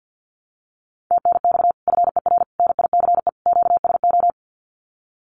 Morse code, heard as a keyed tone, about 50 wpm, 700 Hz. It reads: TNX FER NICE QSO
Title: Common QSO Phrases 50wpm